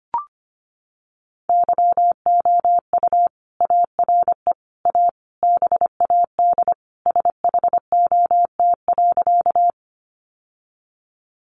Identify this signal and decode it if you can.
Morse code, heard as a keyed tone, about 25 wpm, 700 Hz. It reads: YOU ARE A BAD SHOT.
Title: Sentences from Top 1100 Words 25wpm